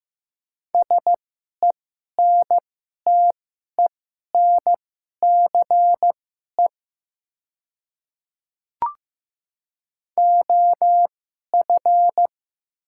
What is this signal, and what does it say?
Morse code, heard as a keyed tone, about 15 wpm, spaced as for 11 wpm, 700 Hz. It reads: SENTENCE OF
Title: 2x Character Spacing - Top 300 Words - 15wpm